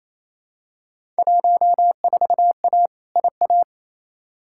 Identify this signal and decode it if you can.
Morse code, heard as a keyed tone, about 28 wpm, 700 Hz. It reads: 14A IA
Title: Field Day Exchanges Encore - 28wpm